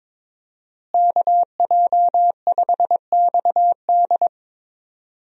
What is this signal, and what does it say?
Morse code, heard as a keyed tone, about 22 wpm, 700 Hz. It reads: KJ5XD
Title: USA Call Signs - 5 Characters - Repeated 2x - 22wpm